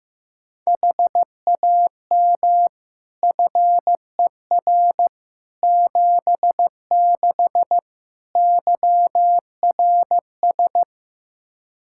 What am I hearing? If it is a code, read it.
Morse code, heard as a keyed tone, about 15 wpm, 700 Hz. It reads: HAM FER 76 YRS
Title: K1USN SST QSO Practice 15wpm